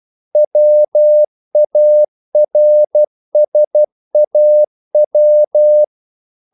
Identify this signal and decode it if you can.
Morse code, heard as a keyed tone, about 12 wpm, 600 Hz. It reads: WARSAW
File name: CW / Practice / NationalCapitals12